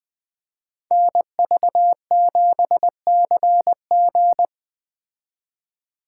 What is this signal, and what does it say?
Morse code, heard as a keyed tone, about 20 wpm, 700 Hz. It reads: NV7CG
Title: Speed Racing - USA 2x2 Call Signs 17wpm